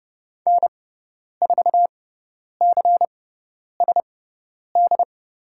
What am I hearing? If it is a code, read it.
Morse code, heard as a keyed tone, about 30 wpm, spaced as for 10 wpm, 700 Hz. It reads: N4CSD
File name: Lesson 18g - Letter m - Call Signs 30wpm